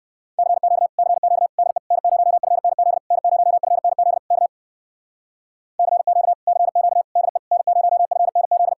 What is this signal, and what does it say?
Morse code, heard as a keyed tone, about 59 wpm, 700 Hz. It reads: CQ CQ DE N0FNZ N0FNZ K CQ CQ DE N0FNZ